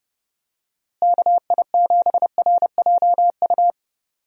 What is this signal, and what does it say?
Morse code, heard as a keyed tone, about 30 wpm, 700 Hz. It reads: KI7RJU
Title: USA Call Signs - 6 Characters - Repeated 3x - 30wpm